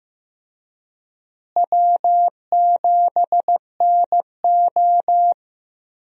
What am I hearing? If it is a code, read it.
Morse code, heard as a keyed tone, about 15 wpm, 700 Hz. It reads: W7NO
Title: USA Call Signs - 4 Characters - Repeated 2x - 15wpm